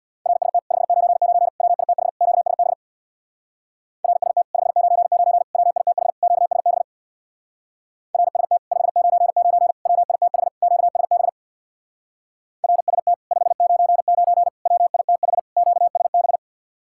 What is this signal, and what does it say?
Morse code, heard as a keyed tone, about 64 wpm, 700 Hz. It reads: RST 599 WITH QSB RST 599 WITH QSB RST 599 WITH QSB RST 599 WITH QSB